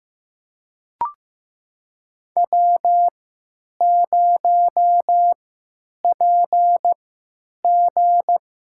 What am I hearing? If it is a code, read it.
Morse code, heard as a keyed tone, about 15 wpm, spaced as for 9 wpm, 700 Hz. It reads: W0PG
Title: USA Call Signs - 3x Character Spacing - 15wpm